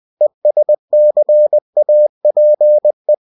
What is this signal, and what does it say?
Morse code, heard as a keyed tone, about 20 wpm, 600 Hz. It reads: ESCAPE
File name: cw-books / winnie-the-pooh / winnie-20wpm-page-0091